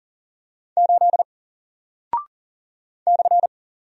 Morse code, heard as a keyed tone, about 40 wpm, 700 Hz. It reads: 8 /
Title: Single Character (Letter:Number:Punctuation:Prosign) 40wpm